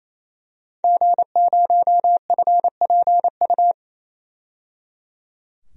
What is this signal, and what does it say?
Morse code, heard as a keyed tone, about 28 wpm, 700 Hz. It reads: G0FPU
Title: DXCC-100 Call Signs - 5 Characters - 28wpm